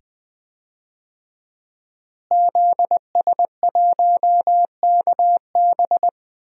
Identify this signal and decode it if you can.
Morse code, heard as a keyed tone, about 20 wpm, 700 Hz. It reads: ZS1KB
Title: DXCC-100 Call Signs - 5 Characters - Repeated 2x - 20wpm